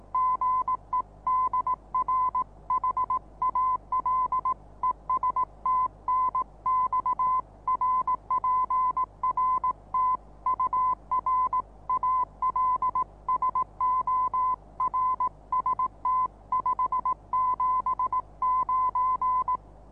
Morse code, heard as a keyed tone, about 18 wpm, 1020 Hz. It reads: GE DR HAL ES TNX RPRT UR ALSO RST 579
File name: repos / Morsekurs / DJ4UF Morsen26